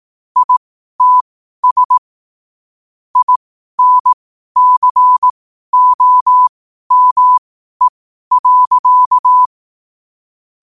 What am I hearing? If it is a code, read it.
Morse code, heard as a keyed tone, about 18 wpm, spaced as for 12 wpm, 1000 Hz. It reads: ITS INCOME.
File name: qotdfiles / 12wpm